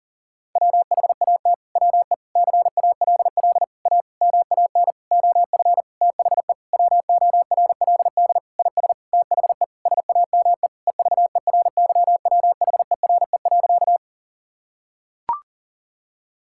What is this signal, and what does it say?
Morse code, heard as a keyed tone, about 40 wpm, 700 Hz. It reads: WHAT WE CALL A MAN OF THE WORLD IS THE SAME EVERYWHERE.
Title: Sentences from Top 1600 Words 40wpm